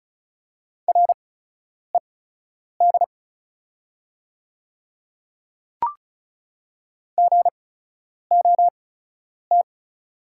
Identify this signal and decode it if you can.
Morse code, heard as a keyed tone, about 36 wpm, spaced as for 10 wpm, 690 Hz. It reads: RED GOT
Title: 8x Character Spacing - Top 500 Words - 35wpm